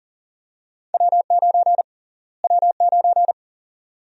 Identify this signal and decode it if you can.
Morse code, heard as a keyed tone, about 40 wpm, 700 Hz. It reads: W9 W9
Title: USA Call Sign Prefixes - Repeated 2x - 40wpm